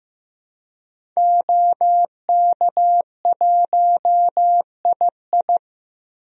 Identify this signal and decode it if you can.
Morse code, heard as a keyed tone, about 16 wpm, 700 Hz. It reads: OK1II
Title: DXCC-100 Call Signs - 5 Characters - Repeated 3x - 15wpm